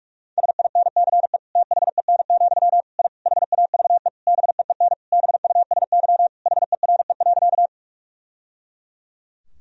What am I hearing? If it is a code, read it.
Morse code, heard as a keyed tone, about 46 wpm, 700 Hz. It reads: SINCE THEN, I HAVE BEEN BUSY HERE.
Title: Sentences from Top 1000 Words - Review Part 3 - 45wpm